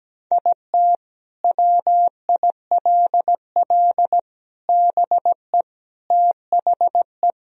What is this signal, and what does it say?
Morse code, heard as a keyed tone, about 17 wpm, 700 Hz. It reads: IT WILL BE THE
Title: Sentences from Top 1000 Words - Review Part 2 - 17wpm